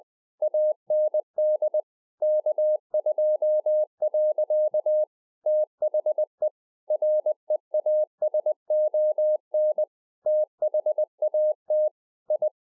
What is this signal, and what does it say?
Morse code, heard as a keyed tone, about 20 wpm, 610 Hz. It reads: AND K2. THE REASON THAT I
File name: 2011-04-mp3 / A SMALL TRANCEIVER 3 20WPM